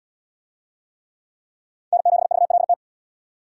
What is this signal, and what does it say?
Morse code, heard as a keyed tone, about 64 wpm, 690 Hz. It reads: N7VLT